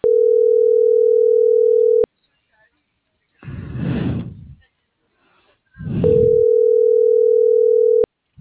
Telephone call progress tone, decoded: ringback tone

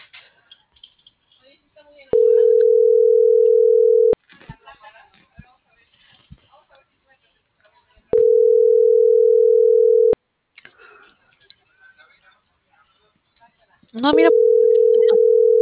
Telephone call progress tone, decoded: ringback tone

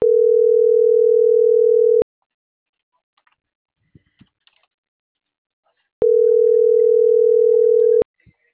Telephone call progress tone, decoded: ringback tone

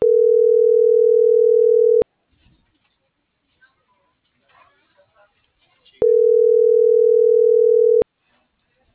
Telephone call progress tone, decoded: ringback tone

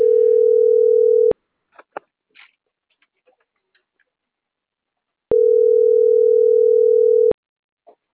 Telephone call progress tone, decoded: ringback tone

